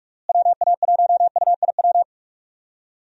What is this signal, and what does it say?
Morse code, heard as a keyed tone, about 46 wpm, 700 Hz. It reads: WA1UIW
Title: USA Call Signs - 6 Characters - Repeated 3x - 45wpm